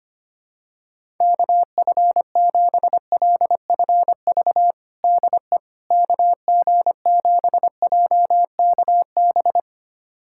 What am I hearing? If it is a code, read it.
Morse code, heard as a keyed tone, about 25 wpm, 700 Hz. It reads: KF7LFV DE KG7JKB
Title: USA Callsign DE Callsign 25wpm